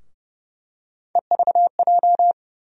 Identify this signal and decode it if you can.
Morse code, heard as a keyed tone, about 30 wpm, 700 Hz. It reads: EVJ
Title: USA Call Sign Postfixes - 3 Letters 30wpm